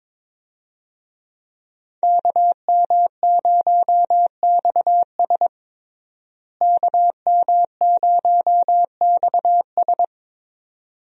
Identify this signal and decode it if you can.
Morse code, heard as a keyed tone, about 22 wpm, 700 Hz. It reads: KM0XS KM0XS
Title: USA Call Signs - 5 Characters - Repeated 2x - 22wpm